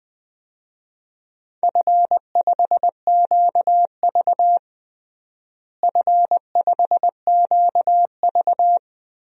Morse code, heard as a keyed tone, about 21 wpm, 700 Hz. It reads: F5QV F5QV
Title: DXCC-100 Call Signs - 4 Characters - Repeated 3x - 20wpm